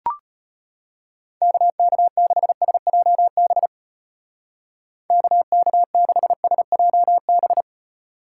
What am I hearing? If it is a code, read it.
Morse code, heard as a keyed tone, about 31 wpm, 700 Hz. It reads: KK6SJB KK6SJB
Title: Speed Racing - USA 2x3 Call Signs 25wpm